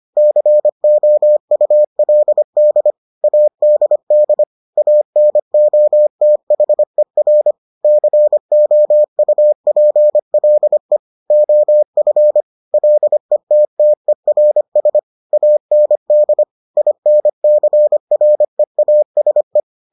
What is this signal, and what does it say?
Morse code, heard as a keyed tone, about 25 wpm, 600 Hz. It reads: COULD ADD ANOTHER COUPLE OF LETTERS AND INCREASE